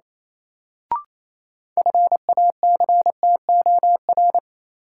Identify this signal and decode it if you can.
Morse code, heard as a keyed tone, about 27 wpm, 700 Hz. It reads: FACTOR
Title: Sentences from Top 1300 Words - New Word Drill Alt - 28wpm